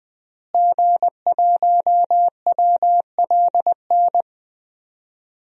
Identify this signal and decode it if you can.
Morse code, heard as a keyed tone, about 20 wpm, 700 Hz. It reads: G1WLN
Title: DXCC-100 Call Signs - 5 Characters - Repeated 3x - 20wpm